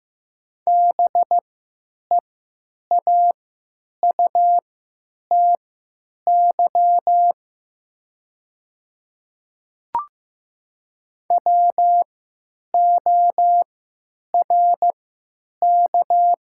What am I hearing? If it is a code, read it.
Morse code, heard as a keyed tone, about 15 wpm, spaced as for 8 wpm, 700 Hz. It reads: BEAUTY WORK
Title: 3x Character Spacing - Top 500 Words - 15wpm